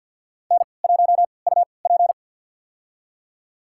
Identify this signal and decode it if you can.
Morse code, heard as a keyed tone, about 52 wpm, spaced as for 29 wpm, 700 Hz. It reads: N1UP
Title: USA Call Signs - 4 Characters - 3x Character Spacing - 50wpm